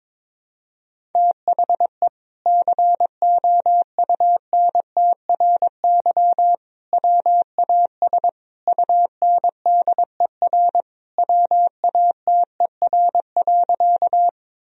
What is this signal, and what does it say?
Morse code, heard as a keyed tone, about 21 wpm, 700 Hz. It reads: THE COUNTRY WAS UNDER WATER.
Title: Sentences from Top 1000 Words - Review Part 1 - 22wpm